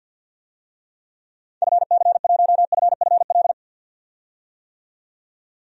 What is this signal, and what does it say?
Morse code, heard as a keyed tone, about 48 wpm, 700 Hz. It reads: FC1FFL
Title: DXCC-100 Call Signs - 6 Characters - 50wpm